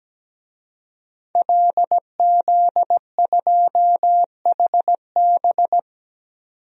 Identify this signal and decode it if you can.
Morse code, heard as a keyed tone, about 17 wpm, 700 Hz. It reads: LZ2HB
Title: DXCC-100 Call Signs - 5 Characters - 17wpm